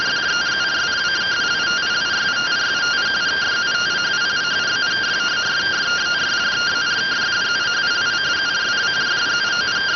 Two-tone teleprinter signal, about 170 Hz shift.